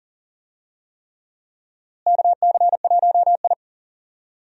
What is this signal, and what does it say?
Morse code, heard as a keyed tone, about 40 wpm, 700 Hz. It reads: KC1I
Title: USA Call Signs - 4 Characters - Repeated 3x - 40wpm